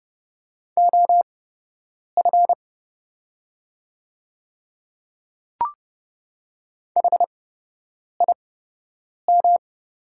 Morse code, heard as a keyed tone, about 31 wpm, spaced as for 8 wpm, 700 Hz. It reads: OF HIM
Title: 8x Character Spacing - Top 200 Words - 30wpm